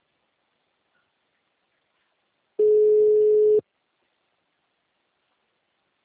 Ringback tone (Europe, 425 Hz).